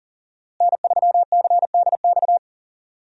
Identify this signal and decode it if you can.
Morse code, heard as a keyed tone, about 40 wpm, 700 Hz. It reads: N3CDX
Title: USA Call Signs - 5 Characters - Repeated 2x - 40wpm